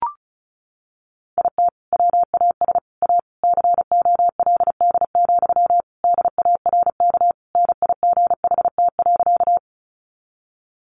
Morse code, heard as a keyed tone, about 34 wpm, 700 Hz. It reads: IT WAS A COLD, DARK NIGHT.
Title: Sentences from Top 500 Words 35wpm